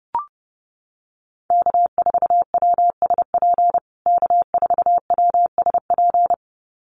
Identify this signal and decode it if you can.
Morse code, heard as a keyed tone, about 30 wpm, 700 Hz. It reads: K4WSP K4WSP
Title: USA Call Signs Repeated 2x 30wpm